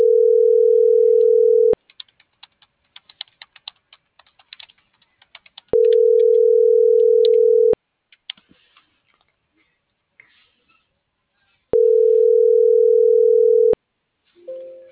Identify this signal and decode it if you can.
Ringback tone (North America), 440 + 480 Hz.